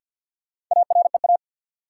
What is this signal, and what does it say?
Morse code, heard as a keyed tone, about 53 wpm, 700 Hz. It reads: AREA
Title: Sentences from Top 1000 Words - Review Part 2 - New Word Drill - 50wpm